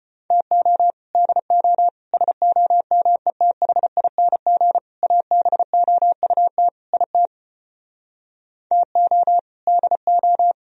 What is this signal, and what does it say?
Morse code, heard as a keyed tone, about 33 wpm, 700 Hz. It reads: TO DO SOMETHING ABOUT IT TO DO